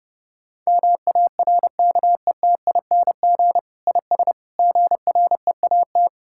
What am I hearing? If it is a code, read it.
Morse code, heard as a keyed tone, about 30 wpm, 700 Hz. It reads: MARKETING IS GREAT
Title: Sentences from Top 1000 Words - Review Part 5 - 30wpm